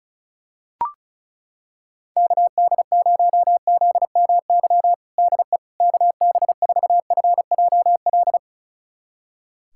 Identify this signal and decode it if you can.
Morse code, heard as a keyed tone, about 35 wpm, 700 Hz. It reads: KD0ZMY DE KB4FJL
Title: USA Callsign DE Callsign - 6 Characters - 35wpm